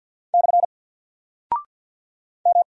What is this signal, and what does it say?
Morse code, heard as a keyed tone, about 48 wpm, 700 Hz. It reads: / M
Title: Single Character (Letter:Number:Punctuation:Prosign) Alt 50wpm